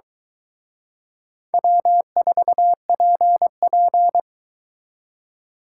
Morse code, heard as a keyed tone, about 22 wpm, 700 Hz. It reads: W4PP